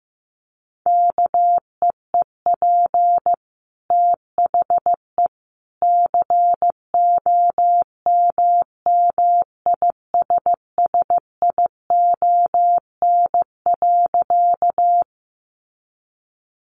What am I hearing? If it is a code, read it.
Morse code, heard as a keyed tone, about 15 wpm, 700 Hz. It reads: KEEP THE COMMISSION.
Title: Sentences from Top 1400 Words 15wpm